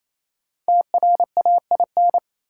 Morse code, heard as a keyed tone, about 28 wpm, 700 Hz. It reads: TRAIN
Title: Five Letter Words - 28wpm